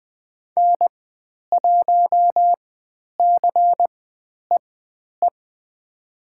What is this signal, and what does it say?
Morse code, heard as a keyed tone, about 20 wpm, spaced as for 10 wpm, 700 Hz. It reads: N1CEE
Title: 102 Lesson 17g - Letter u - Call Signs 20wpm